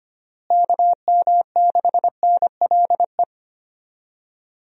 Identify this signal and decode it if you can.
Morse code, heard as a keyed tone, about 25 wpm, 700 Hz. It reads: KM6NLE
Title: USA Call Signs - 6 Characters - Repeated 3x - 25wpm